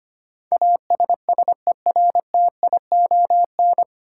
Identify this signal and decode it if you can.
Morse code, heard as a keyed tone, about 25 wpm, 700 Hz. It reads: ASSERTION
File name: Nine Letter Words - 25wpm